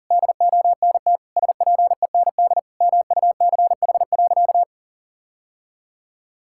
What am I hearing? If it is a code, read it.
Morse code, heard as a keyed tone, about 40 wpm, 700 Hz. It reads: DONT SPEND MUCH.
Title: Sentences from Top 1000 Words - Review Part 5 - 40wpm